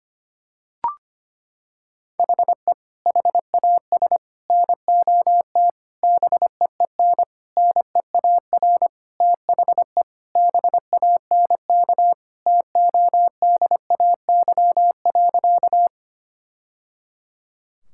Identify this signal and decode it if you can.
Morse code, heard as a keyed tone, about 25 wpm, 700 Hz. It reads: HE HAS NOT BEEN NEAR THE BANK TODAY.
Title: Sentences from Top 1000 Words - Review Part 6 - 25wpm